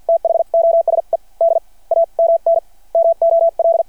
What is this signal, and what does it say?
Morse code, heard as a keyed tone, about 53 wpm, spaced as for 39 wpm, 670 Hz. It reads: THOSE DAMN MOL
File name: archived / CRSnet